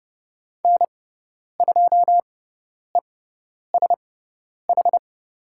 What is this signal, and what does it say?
Morse code, heard as a keyed tone, about 29 wpm, spaced as for 10 wpm, 690 Hz. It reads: N2ESH
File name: Lesson 17g - Letter u - Call Signs 30wpm